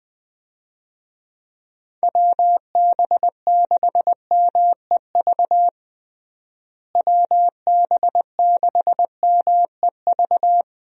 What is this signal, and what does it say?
Morse code, heard as a keyed tone, about 21 wpm, 700 Hz. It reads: WB6MEV WB6MEV
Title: USA Call Signs - 6 Characters - Repeated 3x - 20wpm